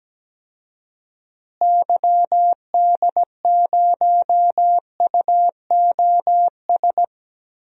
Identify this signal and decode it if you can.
Morse code, heard as a keyed tone, about 17 wpm, 700 Hz. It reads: YD0UOS
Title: DXCC-100 Call Signs - 6 Characters - Repeated 2x - 17wpm